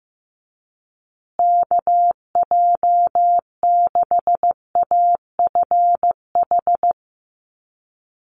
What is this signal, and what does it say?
Morse code, heard as a keyed tone, about 15 wpm, 700 Hz. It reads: KJ6AFH
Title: USA Call Signs Repeated 3x 15wpm